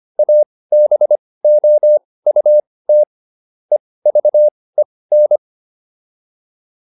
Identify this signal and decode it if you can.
Morse code, heard as a keyed tone, about 25 wpm, spaced as for 18 wpm, 600 Hz. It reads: ABOUT EVEN